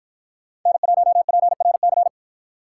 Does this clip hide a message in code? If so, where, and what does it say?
Yes: Morse code from 0.7 to 2.1 s, heard as a keyed tone, about 50 wpm, 700 Hz. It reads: N1PRC